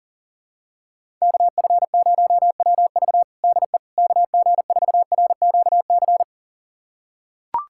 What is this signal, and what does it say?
Morse code, heard as a keyed tone, about 40 wpm, 700 Hz. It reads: KF0WV DE KG4RQC
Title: USA Callsign DE Callsign 40wpm